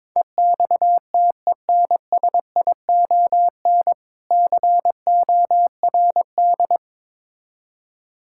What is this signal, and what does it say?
Morse code, heard as a keyed tone, about 22 wpm, 690 Hz. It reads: EXTENSION CORD
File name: Difficult Words with Hints - 22wpm